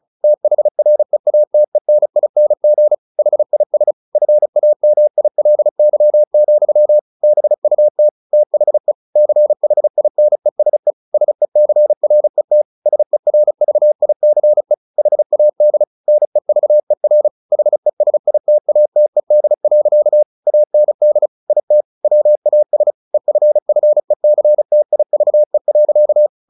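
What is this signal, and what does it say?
Morse code, heard as a keyed tone, about 34 wpm, 600 Hz. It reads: THREATENING HIS FAMILY, BUT THE CHINESE SECRET SERVICE HAD NEVER HESITATED. AND IT WAS EFFECTIVE.
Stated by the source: Morse practice at 35 wpm from CW med Gnister, sending Ken Follett's novel Never in English.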